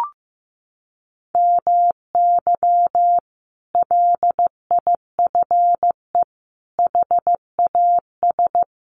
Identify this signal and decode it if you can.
Morse code, heard as a keyed tone, about 15 wpm, 700 Hz. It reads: MY LIFE HAS
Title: Sentences from Top 1100 Words 15wpm